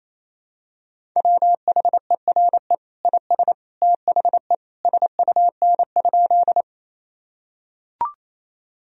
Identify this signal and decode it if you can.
Morse code, heard as a keyed tone, about 28 wpm, 700 Hz. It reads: WHERE IS THE SUN?
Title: Sentences from Top 1000 Words - Review Part 1 - 28wpm